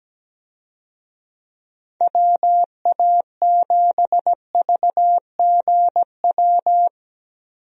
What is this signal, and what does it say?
Morse code, heard as a keyed tone, about 17 wpm, 700 Hz. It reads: WA7VGW
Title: USA Call Signs - 6 Characters - 17wpm